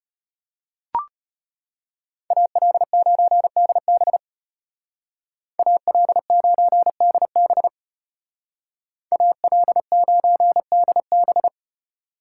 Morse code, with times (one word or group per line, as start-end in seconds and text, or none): 2.3-4.2 AL9DB
5.6-7.7 AL9DB
9.1-11.5 AL9DB